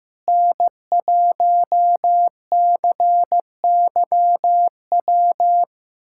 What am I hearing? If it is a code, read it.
Morse code, heard as a keyed tone, about 15 wpm, 700 Hz. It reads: N1CYW
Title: SOTA QSO Practice 15wpm